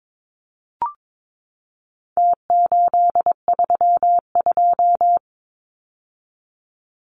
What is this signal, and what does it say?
Morse code, heard as a keyed tone, about 22 wpm, 700 Hz. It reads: T832
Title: Numbers - Contest Style - 4-Digit 22wpm